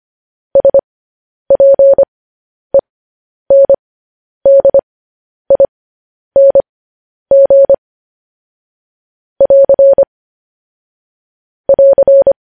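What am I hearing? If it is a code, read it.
Morse code, heard as a keyed tone, about 25 wpm, spaced as for 10 wpm, 550 Hz. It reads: SPENDING <AR> <AR>
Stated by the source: Morse code news headlines practice, at 10 WPM.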